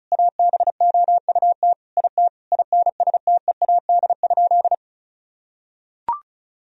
Morse code, heard as a keyed tone, about 36 wpm, 700 Hz. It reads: ABOUT IT INSTEAD?
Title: Sentences from Top 1000 Words - Review Part 4 - 35wpm